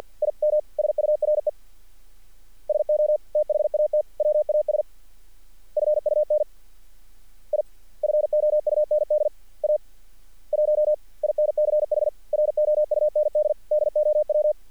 Morse code, heard as a keyed tone, about 50 wpm, 600 Hz. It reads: IM SURE SO THAT WAS FUN I FOUND A 1 INCH ROUND DOW
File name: archived / CRSnet